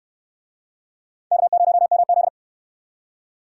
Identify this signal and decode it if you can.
Morse code, heard as a keyed tone, about 68 wpm, 700 Hz. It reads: K0MZ